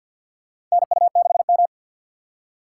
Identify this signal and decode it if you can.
Morse code, heard as a keyed tone, about 53 wpm, 700 Hz. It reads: NU6M
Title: USA Call Signs - 2x Character Spacing - 50wpm